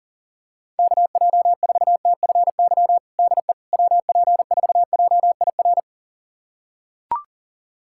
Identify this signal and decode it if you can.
Morse code, heard as a keyed tone, about 40 wpm, 700 Hz. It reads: KJ4TFY DE WP4JIR
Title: USA Callsign DE Callsign 40wpm